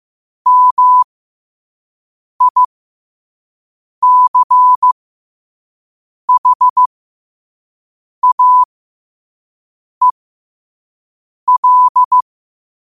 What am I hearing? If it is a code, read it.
Morse code, heard as a keyed tone, about 15 wpm, spaced as for 5 wpm, 1000 Hz. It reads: MICHAEL
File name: qotdfiles / 5wpm